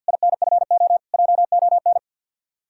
Morse code, heard as a keyed tone, about 60 wpm, 700 Hz. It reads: INFO JON
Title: Speed-Racing Common QSO Phrases - 50wpm